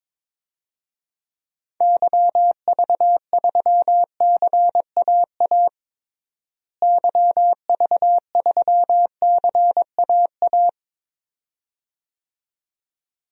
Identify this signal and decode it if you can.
Morse code, heard as a keyed tone, about 22 wpm, 700 Hz. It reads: YV3CAA YV3CAA
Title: DXCC-100 Call Signs - 6 Characters - Repeated 2x - 22wpm